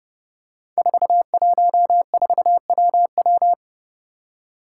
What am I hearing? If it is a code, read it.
Morse code, heard as a keyed tone, about 30 wpm, 700 Hz. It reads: 414WW
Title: Five Challenging Letter-Number Combinations 30wpm